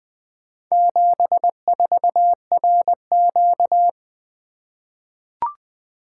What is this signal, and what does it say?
Morse code, heard as a keyed tone, about 20 wpm, 700 Hz. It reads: 74RQ
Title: Four Letter-Number Combinations - 20wpm